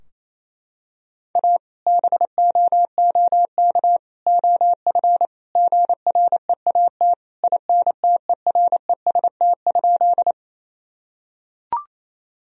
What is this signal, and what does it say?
Morse code, heard as a keyed tone, about 28 wpm, 700 Hz. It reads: A BOOK OF GREAT INTEREST?
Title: Sentences from Top 1000 Words - Review Part 2 - 28wpm